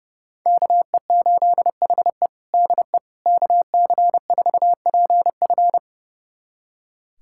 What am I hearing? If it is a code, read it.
Morse code, heard as a keyed tone, about 32 wpm, 700 Hz. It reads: KE8HE DE KC4PF
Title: USA Callsign DE Callsign - 5 Characters - 30wpm